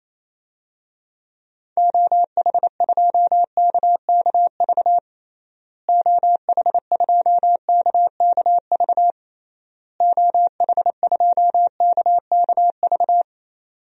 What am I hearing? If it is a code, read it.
Morse code, heard as a keyed tone, about 28 wpm, 700 Hz. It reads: OH2KKV OH2KKV OH2KKV
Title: DXCC-100 Call Signs - 6 Characters - Repeated 3x - 28wpm